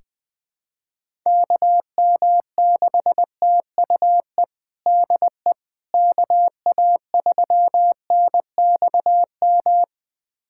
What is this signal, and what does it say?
Morse code, heard as a keyed tone, about 20 wpm, 700 Hz. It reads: KM6TUE DE KA3NXM
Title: USA Callsign DE Callsign 20wpm